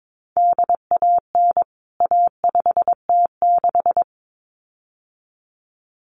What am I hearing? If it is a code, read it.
Morse code, heard as a keyed tone, about 22 wpm, 700 Hz. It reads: DAN A5T6
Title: CWT Exchange 22wpm